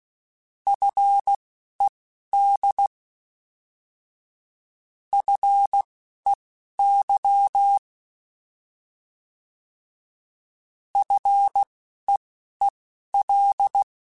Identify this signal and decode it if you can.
Morse code, heard as a keyed tone, about 16 wpm, spaced as for 11 wpm, 800 Hz. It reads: FED FEY FEEL
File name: Alphabetical / Alphabetical mp3s / Fs @10